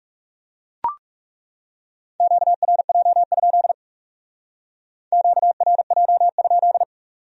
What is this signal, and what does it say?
Morse code, heard as a keyed tone, about 36 wpm, 700 Hz. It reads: QRJ? QRJ?
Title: Speed-Racing Q-Codes - 30wpm